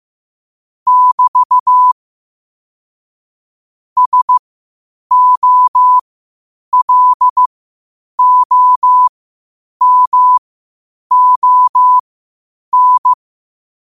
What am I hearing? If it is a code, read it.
Morse code, heard as a keyed tone, about 15 wpm, spaced as for 8 wpm, 1000 Hz. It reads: <BT> SOLOMON
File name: qotdfiles / 8wpm